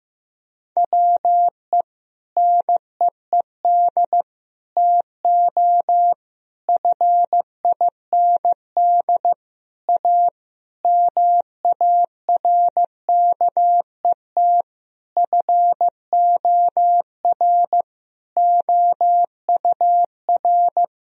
Morse code, heard as a keyed tone, about 15 wpm, 700 Hz. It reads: WE NEED TO FIND A MARKET FOR OUR
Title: Sentences from Top 1000 Words - Review Part 3 - 15wpm